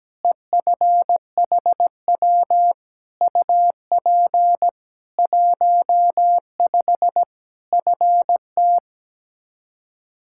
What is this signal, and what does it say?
Morse code, heard as a keyed tone, about 17 wpm, 700 Hz. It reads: EFHW UP 15 FT
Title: K1USN SST QSO Practice 17wpm